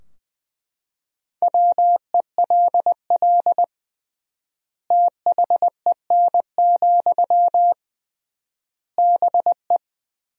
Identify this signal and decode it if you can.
Morse code, heard as a keyed tone, about 20 wpm, 700 Hz. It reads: WELL THEN, BE
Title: 3x Word-Spacing - Sentences from Top 500 Words - 20wpm